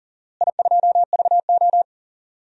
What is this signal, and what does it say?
Morse code, heard as a keyed tone, about 40 wpm, 690 Hz. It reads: I2VO
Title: DXCC-100 Call Signs - 4 Characters - Repeated 2x - 40wpm